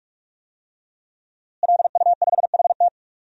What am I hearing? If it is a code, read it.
Morse code, heard as a keyed tone, about 42 wpm, 700 Hz. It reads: LU5HT
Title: DXCC-100 Call Signs - 5 Characters - Repeated 2x - 45wpm